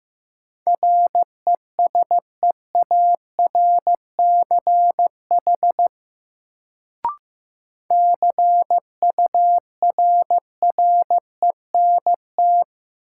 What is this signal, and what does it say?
Morse code, heard as a keyed tone, about 15 wpm, 700 Hz. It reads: RESEARCH CURRENT
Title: Sentences from Top 1000 Words - Review Part 4 - New Word Drill - 15wpm